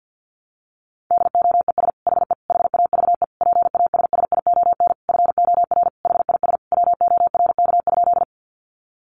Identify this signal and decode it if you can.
Morse code, heard as a keyed tone, about 48 wpm, 700 Hz. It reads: DOES HE HAVE PASSION FOR HIS WORK?
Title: Sentences from Top 1300 Words 50wpm